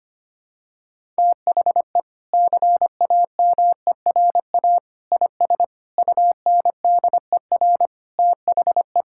Morse code, heard as a keyed tone, about 25 wpm, 700 Hz. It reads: THE CAMERA IS UNDER THE
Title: Sentences from Top 1000 Words - Review Part 5 - 25wpm